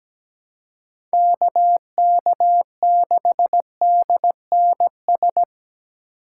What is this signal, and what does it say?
Morse code, heard as a keyed tone, about 20 wpm, 700 Hz. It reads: KK6DNS
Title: Speed Racing - USA 2x3 Call Signs 17wpm